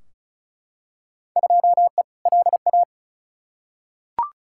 Morse code, heard as a keyed tone, about 36 wpm, 700 Hz. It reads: 2E LA
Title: Field Day Exchanges Encore - 35wpm